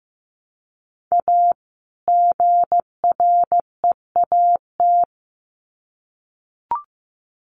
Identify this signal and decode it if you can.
Morse code, heard as a keyed tone, about 15 wpm, 700 Hz. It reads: A GREAT
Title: Sets of 2 Words 15wpm